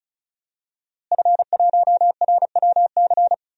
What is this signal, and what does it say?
Morse code, heard as a keyed tone, about 36 wpm, 700 Hz. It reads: F1RWC
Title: DXCC-100 Call Signs - 5 Characters - Repeated 3x - 35wpm